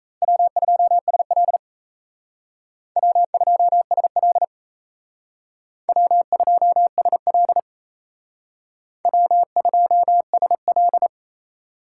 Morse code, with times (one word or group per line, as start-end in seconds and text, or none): 0.2-1.6 W2SL
3.0-4.4 W2SL
5.9-7.6 W2SL
9.1-11.1 W2SL